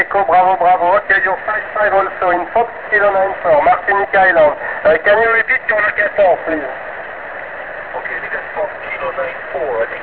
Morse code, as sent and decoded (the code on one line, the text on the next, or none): none